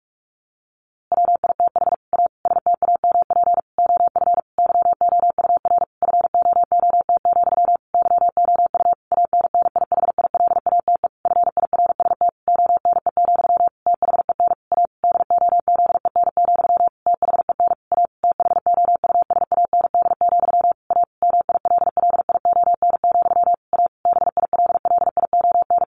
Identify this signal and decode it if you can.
Morse code, heard as a keyed tone, about 44 wpm, 700 Hz. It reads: WITH A STAMP OF YOUR FOOT, YOU ANNIHILATE FIRST ONE, THEN A DOZEN, THEN A THOUSAND, A MILLION, A BILLION
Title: A Sound Of Thunder 45wpm